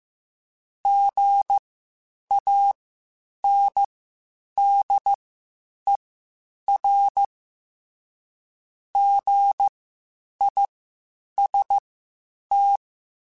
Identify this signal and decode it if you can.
Morse code, heard as a keyed tone, about 15 wpm, spaced as for 8 wpm, 800 Hz. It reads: GANDER GIST
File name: Spaced Alphas / Spaced Alphas mp3s / Gs sp @13